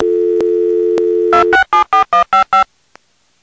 DTMF, heard as telephone touch tones, tones of about 120 ms. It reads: 2674155